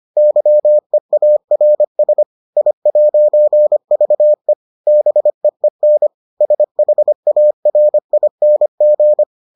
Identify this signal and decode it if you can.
Morse code, heard as a keyed tone, about 25 wpm, 600 Hz. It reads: YEARS I'VE BEEN SHARING